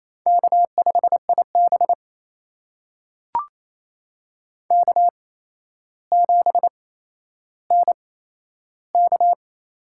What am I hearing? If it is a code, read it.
Morse code, heard as a keyed tone, about 28 wpm, spaced as for 12 wpm, 700 Hz. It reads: K5IB K7NK
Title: USA Call Signs - 4 Characters - 8x Character Spacing - 28wpm